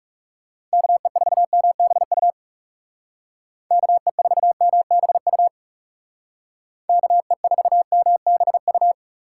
Morse code, heard as a keyed tone, about 39 wpm, 700 Hz. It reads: KE4MBU KE4MBU KE4MBU